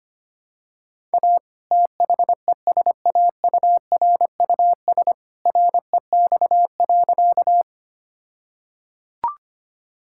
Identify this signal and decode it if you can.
Morse code, heard as a keyed tone, about 25 wpm, 700 Hz. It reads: A THESAURUS REX.
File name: practice / 25wpm / All-Age Appropriate Jokes - 25wpm